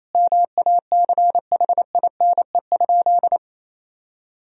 Morse code, heard as a keyed tone, about 27 wpm, 700 Hz. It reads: MACHINE?
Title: Sentences from Top 1000 Words - Review Part 2 - 28wpm